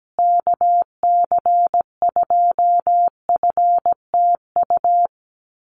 Morse code, heard as a keyed tone, about 17 wpm, 700 Hz. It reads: KC2FTU
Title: USA Call Signs Repeated 2x 17wpm